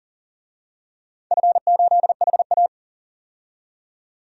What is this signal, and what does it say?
Morse code, heard as a keyed tone, about 40 wpm, 700 Hz. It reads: F8HA
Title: DXCC-100 Call Signs - 4 Characters - 40wpm